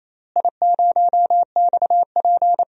Morse code, heard as a keyed tone, about 27 wpm, 700 Hz. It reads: I0XP
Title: DXCC-100 Call Signs - 4 Characters - Repeated 3x - 28wpm